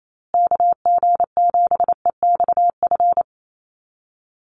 Morse code, heard as a keyed tone, about 28 wpm, 700 Hz. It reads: KG7EXF
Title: USA Call Signs Repeated 3x 28wpm